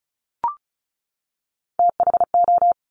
Morse code, heard as a keyed tone, about 36 wpm, 700 Hz. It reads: THO
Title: Top Three Letter Combinations in Words 35wpm